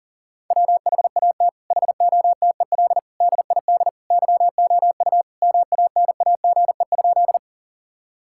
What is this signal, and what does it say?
Morse code, heard as a keyed tone, about 40 wpm, 700 Hz. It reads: WHAT HOTEL DID YOU MANAGE?
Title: Sentences from Top 1600 Words 40wpm